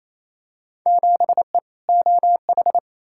Morse code, heard as a keyed tone, about 27 wpm, 700 Hz. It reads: 7E OH
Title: Field Day Exchanges Encore - 28wpm